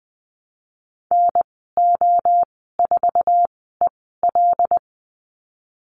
Morse code, heard as a keyed tone, about 20 wpm, 700 Hz. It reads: N O 4 E L
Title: 076 Lesson 13g - Letter L - Call Signs 20wpm